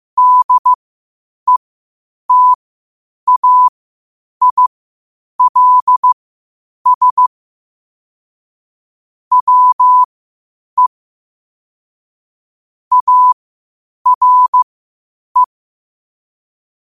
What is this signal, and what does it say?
Morse code, heard as a keyed tone, about 15 wpm, spaced as for 8 wpm, 1000 Hz. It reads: DETAILS WE ARE
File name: qotdfiles / 8wpm